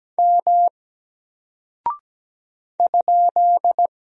Morse code, heard as a keyed tone, about 17 wpm, 700 Hz. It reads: M ?
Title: Single Character (Letter:Number:Punctuation:Prosign) 17wpm